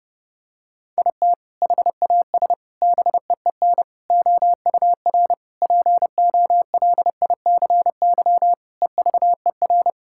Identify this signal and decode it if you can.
Morse code, heard as a keyed tone, about 30 wpm, 700 Hz. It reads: IT HAS BEEN OUR POLICY EVER
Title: Sentences from Top 1000 Words - Review Part 5 - 30wpm